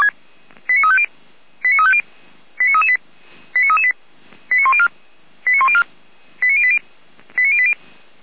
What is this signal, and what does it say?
Five-tone selective call (ZVEI), heard as quick runs of five tones, tones of about 80 ms. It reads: 89350 89350 89338 89338 89104 89104 89080 89080